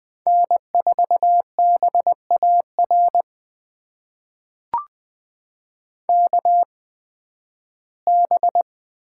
Morse code, heard as a keyed tone, about 20 wpm, spaced as for 13 wpm, 700 Hz. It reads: N4BAR KB
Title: USA Call Signs - 8x Character Spacing - 20wpm